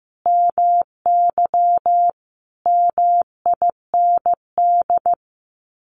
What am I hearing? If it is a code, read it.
Morse code, heard as a keyed tone, about 15 wpm, 700 Hz. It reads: MY MIND
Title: Sets of 5 Words Encore2 15wpm